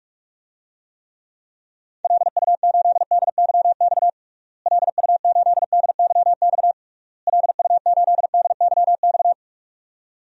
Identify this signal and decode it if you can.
Morse code, heard as a keyed tone, about 41 wpm, 700 Hz. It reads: LU8DYX LU8DYX LU8DYX
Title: DXCC-100 Call Signs - 6 Characters - Repeated 3x - 45wpm